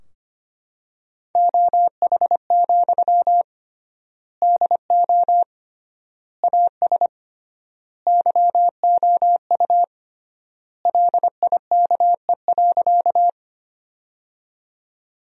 Morse code, heard as a keyed tone, about 25 wpm, 700 Hz. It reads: OH, DO AS YOU LIKE.
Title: 3x Word-Spacing - Sentences from Top 500 Words - 25wpm